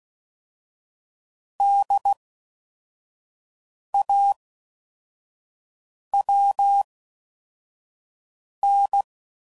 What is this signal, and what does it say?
Morse code, heard as a keyed tone, about 16 wpm, spaced as for 4 wpm, 800 Hz. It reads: DAWN